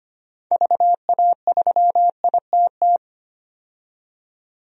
Morse code, heard as a keyed tone, about 25 wpm, 700 Hz. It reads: VA3ITT
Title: DXCC-100 Call Signs - 6 Characters - 25wpm